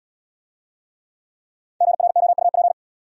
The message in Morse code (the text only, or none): GROUP